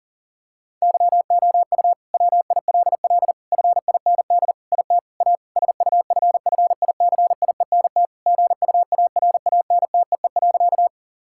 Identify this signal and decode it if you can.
Morse code, heard as a keyed tone, about 40 wpm, 700 Hz. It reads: YOU WILL FIND IT A SUFFICIENT GUARANTEE.
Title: Sentences from Top 1600 Words 40wpm